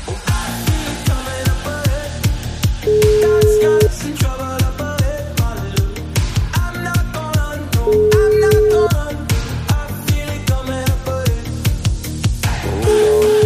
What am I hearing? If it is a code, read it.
Ringback tone (Europe, 425 Hz), with noise.